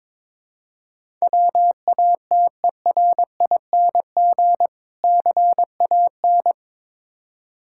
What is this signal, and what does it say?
Morse code, heard as a keyed tone, about 22 wpm, 690 Hz. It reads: WATERING CAN
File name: Difficult Words with Hints - 22wpm